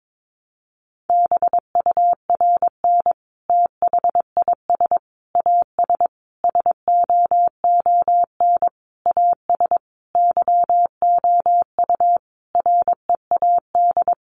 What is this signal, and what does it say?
Morse code, heard as a keyed tone, about 22 wpm, 700 Hz. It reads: BURN THIS AS SOON AS YOU READ
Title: Sentences from Top 1300 Words 22wpm